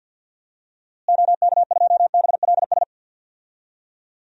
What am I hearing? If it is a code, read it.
Morse code, heard as a keyed tone, about 47 wpm, 700 Hz. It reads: OK2BLS
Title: DXCC-100 Call Signs - 6 Characters - Repeated 2x - 50wpm